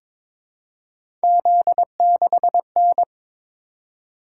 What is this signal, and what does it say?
Morse code, heard as a keyed tone, about 22 wpm, 700 Hz. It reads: Z6N
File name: practice / 22wpm / Three Letter-Number Combinations - 22wpm